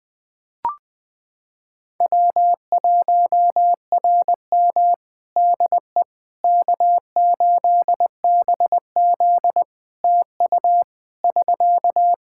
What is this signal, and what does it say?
Morse code, heard as a keyed tone, about 20 wpm, 700 Hz. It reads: W1RM DE K8BZ TU <SK>
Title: Practice QSO - Exchange 4 - Ending - 20wpm